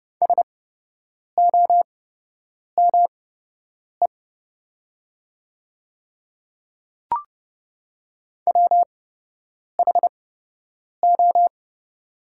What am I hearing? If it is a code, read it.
Morse code, heard as a keyed tone, about 29 wpm, spaced as for 8 wpm, 700 Hz. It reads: SOME WHO
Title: 8x Character Spacing - Top 100 Words - 30wpm